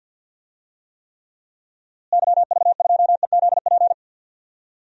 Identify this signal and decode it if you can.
Morse code, heard as a keyed tone, about 50 wpm, 700 Hz. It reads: YV2EZP
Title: DXCC-100 Call Signs - 6 Characters - Repeated 3x - 50wpm